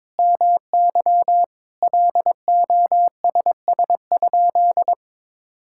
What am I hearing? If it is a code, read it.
Morse code, heard as a keyed tone, about 22 wpm, 700 Hz. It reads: MY LOSS?
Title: Sentences from Top 1600 Words 22wpm